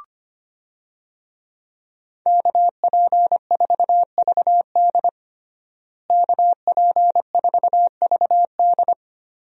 Morse code, heard as a keyed tone, about 25 wpm, 700 Hz. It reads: KP4VD KP4VD
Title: DXCC-100 Call Signs - 5 Characters - Repeated 3x - 25wpm